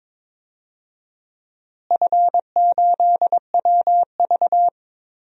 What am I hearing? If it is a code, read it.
Morse code, heard as a keyed tone, about 22 wpm, 700 Hz. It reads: F8WV
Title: DXCC-100 Call Signs - 4 Characters - Repeated 3x - 22wpm